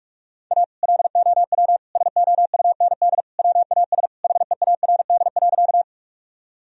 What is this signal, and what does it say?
Morse code, heard as a keyed tone, about 44 wpm, 700 Hz. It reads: A LOW SOUND WAS HEARD.
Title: Sentences from Top 1000 Words - Review Part 1 - 45wpm